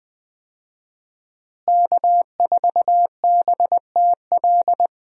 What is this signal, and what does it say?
Morse code, heard as a keyed tone, about 20 wpm, 700 Hz. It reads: K4BTL
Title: USA Call Signs - 5 Characters - Repeated 3x - 20wpm